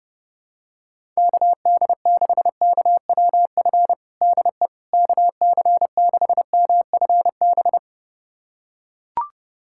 Morse code, heard as a keyed tone, about 30 wpm, 700 Hz. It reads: KD6KWF DE KC6MFB
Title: USA Callsign DE Callsign - 6 Characters - 30wpm